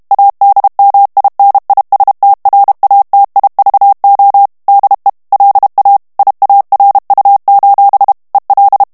Morse code, heard as a keyed tone, about 32 wpm, 800 Hz. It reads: ADMINISTRATIVO DE LA IARU: EL
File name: morse mp3 / ESPAGNOL TEXTE EN CLAIR 1 VITESSE 30 MOTS MINUTES